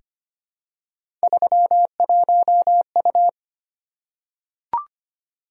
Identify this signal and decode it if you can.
Morse code, heard as a keyed tone, about 25 wpm, 700 Hz. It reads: 31U
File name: Three Letter-Number Combinations - 25wpm